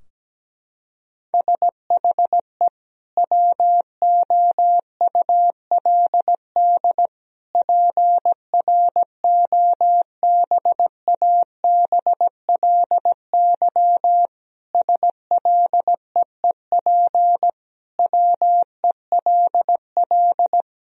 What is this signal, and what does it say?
Morse code, heard as a keyed tone, about 17 wpm, 700 Hz. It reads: SHE WOULD PROBABLY SLEEP WELL
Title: Sentences from Top 1000 Words - Review Part 4 - 17wpm